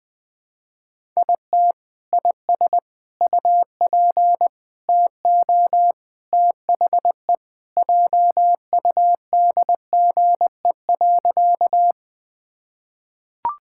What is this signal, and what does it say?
Morse code, heard as a keyed tone, about 20 wpm, 700 Hz. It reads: IT IS UP TO THE JUDGE.
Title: Sentences with 3-Word Phrases - 20wpm